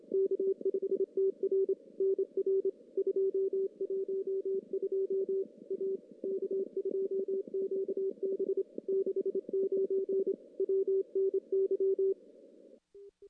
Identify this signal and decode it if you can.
Morse code, heard as a keyed tone, about 27 wpm, 390 Hz. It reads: K5TR NR 212 A K2QB 69 WNY
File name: contests / CW-SS-k5tr- -1600